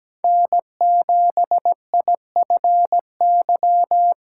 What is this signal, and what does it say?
Morse code, heard as a keyed tone, about 17 wpm, 700 Hz. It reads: N7IFY
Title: USA Callsign DE Callsign - 5 Characters - 17wpm